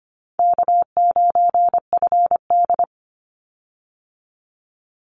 Morse code, heard as a keyed tone, about 25 wpm, 700 Hz. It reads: K9FD
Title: Call Signs Repeated 3x - 25wpm